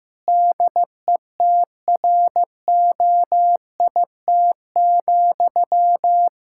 Morse code, heard as a keyed tone, about 15 wpm, 700 Hz. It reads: DETROIT,
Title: Practice QSO - Exchange 1 - Beginning - 15wpm